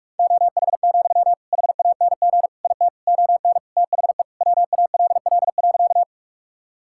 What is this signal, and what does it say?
Morse code, heard as a keyed tone, about 46 wpm, 700 Hz. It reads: OH, HANG IT ON THE WALL.
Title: Sentences from Top 1600 Words 45wpm